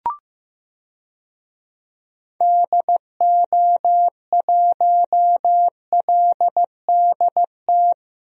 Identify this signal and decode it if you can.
Morse code, heard as a keyed tone, about 16 wpm, 700 Hz. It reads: DO1LDT
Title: DXCC-100 Call Signs - 6 Characters - Repeated 2x - 15wpm